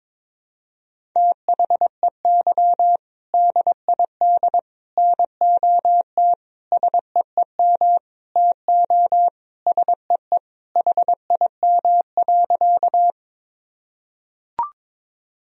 Morse code, heard as a keyed tone, about 22 wpm, 700 Hz. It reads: THEY DID NOT SEEM TO SEE HIM.
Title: Sentences from Top 1000 Words - Review Part 1 - 22wpm